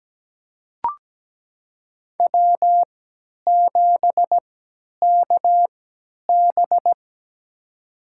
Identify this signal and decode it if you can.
Morse code, heard as a keyed tone, about 17 wpm, spaced as for 10 wpm, 700 Hz. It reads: W7KB
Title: USA Call Signs - 4 Characters - 3x Character Spacing - 17wpm